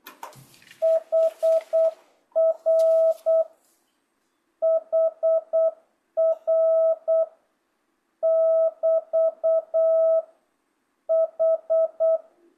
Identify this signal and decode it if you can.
Morse code, heard as a keyed tone, about 8 wpm, 660 Hz. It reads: HR HR <BT> H
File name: jh8cbh / 2025cw